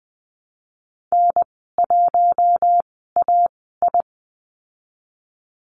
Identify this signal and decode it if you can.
Morse code, heard as a keyed tone, about 20 wpm, spaced as for 14 wpm, 700 Hz. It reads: N1AI
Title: 048 Lesson 9g - Number 4 - Call Signs 20wpm